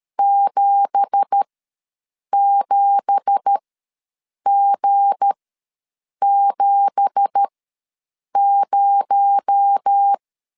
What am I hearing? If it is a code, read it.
Morse code, heard as a keyed tone, about 13 wpm, spaced as for 7 wpm, 800 Hz. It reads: 77G70
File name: LEZIONI / 9 2867 ws 12wpm 3431